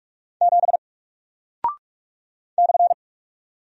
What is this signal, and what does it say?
Morse code, heard as a keyed tone, about 46 wpm, 700 Hz. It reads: 7 /
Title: Single Character (Letter:Number:Punctuation:Prosign) Alt 45wpm